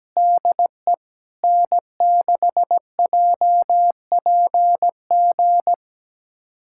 Morse code, heard as a keyed tone, about 17 wpm, 700 Hz. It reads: DE N6JPG
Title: USA Callsign DE Callsign - 5 Characters - 17wpm